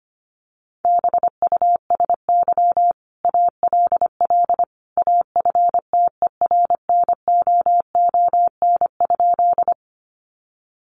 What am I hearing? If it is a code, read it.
Morse code, heard as a keyed tone, about 25 wpm, 700 Hz. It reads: BUSY ALL AFTERNOON?
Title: Sentences from Top 1500 Words 25wpm